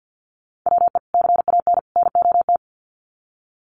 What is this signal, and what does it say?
Morse code, heard as a keyed tone, about 50 wpm, 700 Hz. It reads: WE CAN NOT
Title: Sets of 3 Words 50wpm